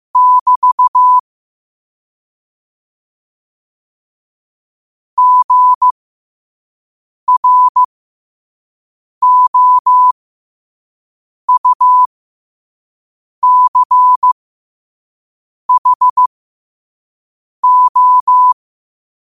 Morse code, heard as a keyed tone, about 15 wpm, spaced as for 5 wpm, 1000 Hz. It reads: <BT> GROUCHO
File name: qotdfiles / 5wpm